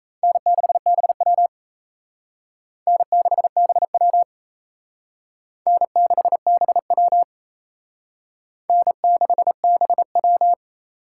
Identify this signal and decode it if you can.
Morse code, heard as a keyed tone, about 34 wpm, 700 Hz. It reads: N6BW N6BW N6BW N6BW
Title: Speed Racing - USA 1x2 Call Signs 28wpm